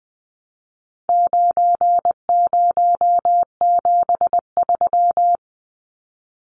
Morse code, heard as a keyed tone, about 20 wpm, 700 Hz. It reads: 9073
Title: Numbers - 4-Digit 20wpm